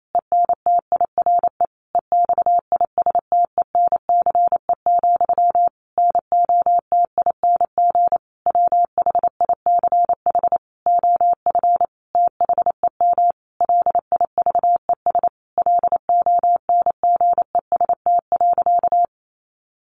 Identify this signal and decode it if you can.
Morse code, heard as a keyed tone, about 28 wpm, 700 Hz. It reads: ENTIRE EXISTENCE, NOTING WHICH OF THEM LIVES LONGEST.
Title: A Sound Of Thunder 28wpm